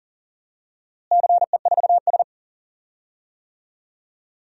Morse code, heard as a keyed tone, about 40 wpm, 700 Hz. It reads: CE4S
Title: DXCC-100 Call Signs - 4 Characters - Repeated 2x - 40wpm